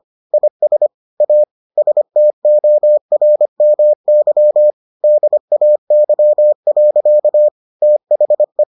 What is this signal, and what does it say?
Morse code, heard as a keyed tone, about 25 wpm, 600 Hz. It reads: IS A STORMY DAY. THE